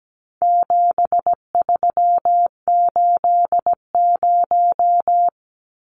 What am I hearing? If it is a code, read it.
Morse code, heard as a keyed tone, about 17 wpm, 700 Hz. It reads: 7380
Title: Numbers - 4-Digit 17wpm